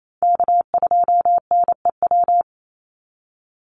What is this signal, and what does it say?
Morse code, heard as a keyed tone, about 28 wpm, 700 Hz. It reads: K2NEW
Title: USA Call Signs 28wpm